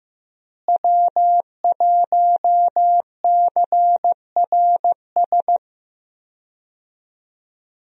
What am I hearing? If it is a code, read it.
Morse code, heard as a keyed tone, about 15 wpm, 700 Hz. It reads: W1CRS
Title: USA Call Signs - 5 Characters - Repeated 3x - 15wpm